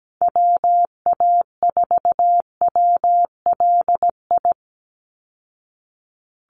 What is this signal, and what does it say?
Morse code, heard as a keyed tone, about 17 wpm, 700 Hz. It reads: WA4WLI
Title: Call Signs Repeated 2x - 17wpm